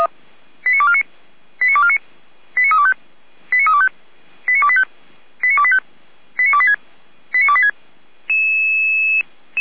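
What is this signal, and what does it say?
Five-tone selective call (ZVEI), heard as quick runs of five tones, tones of about 70 ms. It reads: 89249 89249 89325 89325 89285 89285 89286 89286
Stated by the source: Five-tone ZVEI selective call received on SDR, from the feuerwehr-orlamuende.org fire brigade audio server.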